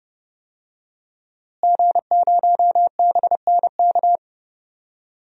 Morse code, heard as a keyed tone, about 30 wpm, 700 Hz. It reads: G0BNK
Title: DXCC-100 Call Signs - 5 Characters - Repeated 2x - 30wpm